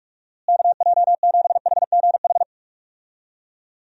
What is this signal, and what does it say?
Morse code, heard as a keyed tone, about 46 wpm, 700 Hz. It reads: KJ7HGH
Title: USA Call Signs - 6 Characters - Repeated 2x - 45wpm